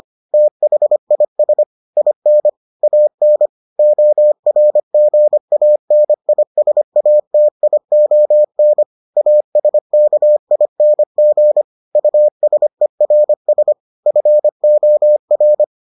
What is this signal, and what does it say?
Morse code, heard as a keyed tone, about 25 wpm, 600 Hz. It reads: THIS IN AN ORGANISATION ASKING USERS FOR